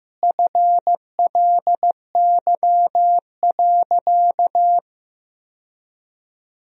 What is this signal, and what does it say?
Morse code, heard as a keyed tone, about 15 wpm, 700 Hz. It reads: FLY.
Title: Sentences from Top 1000 Words - Review Part 2 - 15wpm